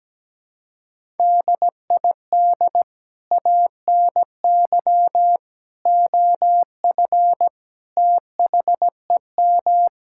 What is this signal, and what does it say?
Morse code, heard as a keyed tone, about 17 wpm, 700 Hz. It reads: DID ANY OF THEM
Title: Sentences from Top 1600 Words 17wpm